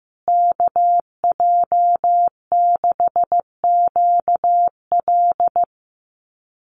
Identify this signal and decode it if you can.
Morse code, heard as a keyed tone, about 15 wpm, 700 Hz. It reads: KJ6QL
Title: USA Call Signs Repeated 3x 15wpm